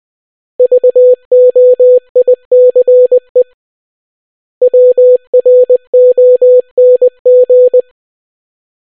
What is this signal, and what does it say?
Morse code, heard as a keyed tone, about 20 wpm, 500 Hz. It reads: VOICE WRONG